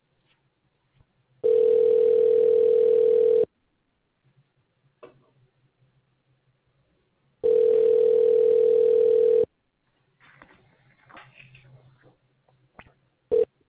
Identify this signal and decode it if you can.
Ringback tone (North America), 440 + 480 Hz.
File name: recordings / REadf5828df3f86031b250fa15447a9528